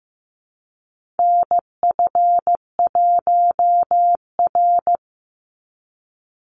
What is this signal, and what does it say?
Morse code, heard as a keyed tone, about 15 wpm, 700 Hz. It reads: NF1R
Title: NA SOTA Chaser Callsigns 15wpm